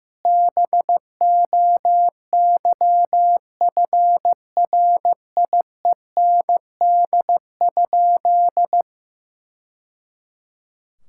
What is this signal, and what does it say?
Morse code, heard as a keyed tone, about 14 wpm, 700 Hz. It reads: BOYFRIEND?
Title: Sentences from Top 1000 Words - Review Part 5 - 15wpm